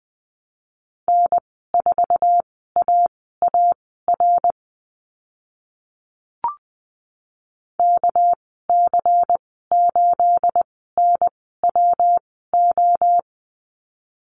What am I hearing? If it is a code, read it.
Morse code, heard as a keyed tone, about 20 wpm, spaced as for 14 wpm, 700 Hz. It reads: N4AAR KC8NWO
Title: 219 Lesson 35g - Number 0 - Call Signs 20wpm